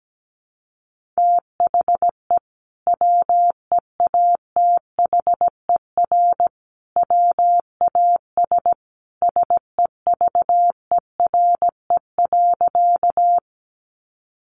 Morse code, heard as a keyed tone, about 17 wpm, 700 Hz. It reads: THE WEATHER WAS SEVERE.
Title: Sentences from Top 1400 Words 17wpm